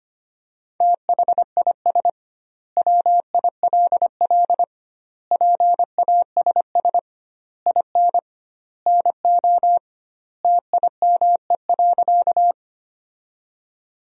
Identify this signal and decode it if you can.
Morse code, heard as a keyed tone, about 25 wpm, 700 Hz. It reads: THIS WILL PASS IN NO TIME.
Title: 2x Word-Spacing - Sentences from Top 400 Words - 25wpm